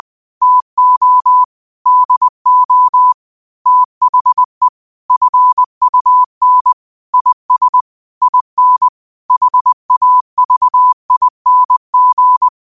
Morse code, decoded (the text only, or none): TO DO THE FUN IS IN HAVING